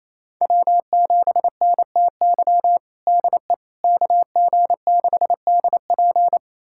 Morse code, heard as a keyed tone, about 28 wpm, 700 Hz. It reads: W7NTY DE KG6DP
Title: USA Callsign DE Callsign - 5 Characters - 28wpm